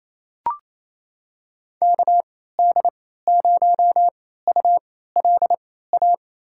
Morse code, heard as a keyed tone, about 28 wpm, spaced as for 16 wpm, 700 Hz. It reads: KD0ULA
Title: USA Call Signs - 3x Character Spacing - 28wpm